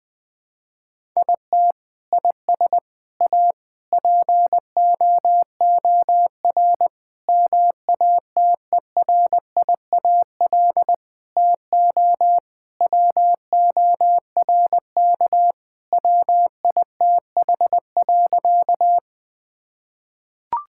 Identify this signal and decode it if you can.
Morse code, heard as a keyed tone, about 20 wpm, 700 Hz. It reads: IT IS A POOR MATERIAL TO WORK WITH.
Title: Sentences from Top 1000 Words - Review Part 5 - 20wpm